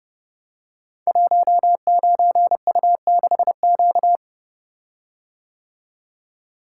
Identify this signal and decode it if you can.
Morse code, heard as a keyed tone, about 30 wpm, 700 Hz. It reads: 19U6Q
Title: Code Groups - 5 Letter:Numbers - 30wpm